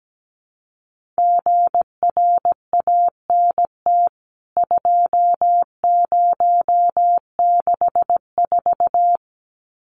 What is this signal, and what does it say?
Morse code, heard as a keyed tone, about 17 wpm, 700 Hz. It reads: GRANT 2064
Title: CWT Exchange 17wpm